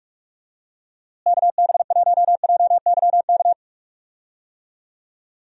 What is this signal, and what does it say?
Morse code, heard as a keyed tone, about 42 wpm, 700 Hz. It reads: KB1JYK
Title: USA Call Signs - 6 Characters - Repeated 2x - 45wpm